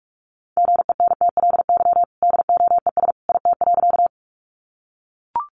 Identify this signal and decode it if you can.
Morse code, heard as a keyed tone, about 44 wpm, 700 Hz. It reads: GENTLY DOES IT.
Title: Sentences from Top 1500 Words 45wpm